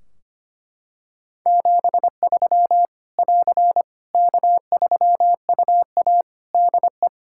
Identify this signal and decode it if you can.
Morse code, heard as a keyed tone, about 25 wpm, 700 Hz. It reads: 73 <AR> K3UA DE
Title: K1USN SST QSO Practice 25wpm